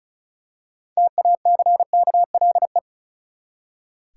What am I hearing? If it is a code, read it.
Morse code, heard as a keyed tone, about 36 wpm, 700 Hz. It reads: TACKLE
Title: Six Letter Words - 35wpm